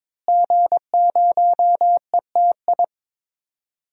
Morse code, heard as a keyed tone, about 22 wpm, 700 Hz. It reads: G0ETI
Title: DXCC-100 Call Signs - 5 Characters - Repeated 3x - 22wpm